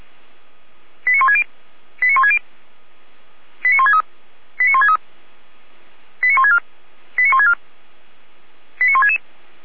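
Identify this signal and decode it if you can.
Five-tone selective call (ZVEI), heard as quick runs of five tones, tones of about 70 ms. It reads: 89160 89160 89163 89163 89164 89164 89166
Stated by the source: ZVEI five-tone call 89130 received on SDR.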